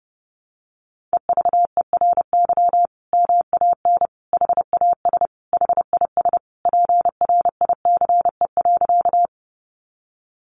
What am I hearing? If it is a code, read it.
Morse code, heard as a keyed tone, about 30 wpm, 700 Hz. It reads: EVERY MAN HAS HIS PRICE.
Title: Sentences from Top 600 Words 30wpm